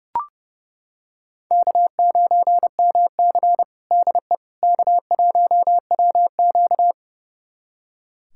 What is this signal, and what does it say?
Morse code, heard as a keyed tone, about 32 wpm, 700 Hz. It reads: K9MC DE K1WQ
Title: USA Callsign DE Callsign - 4 Characters - 30wpm